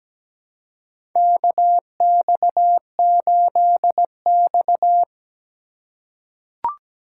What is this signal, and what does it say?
Morse code, heard as a keyed tone, about 17 wpm, 700 Hz. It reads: KX8X
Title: K1USN SST QSO Practice - 17wpm